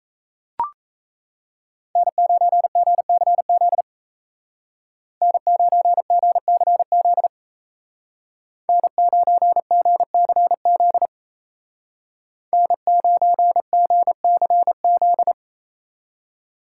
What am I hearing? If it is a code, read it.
Morse code, heard as a keyed tone, about 33 wpm, 700 Hz. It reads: N9GCZ N9GCZ N9GCZ N9GCZ
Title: Speed Racing - USA 1x3 Call Signs 28wpm